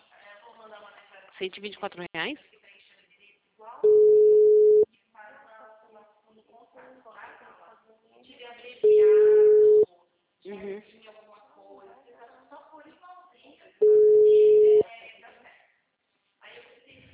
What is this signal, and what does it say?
Ringback tone (Europe, 425 Hz), with speech.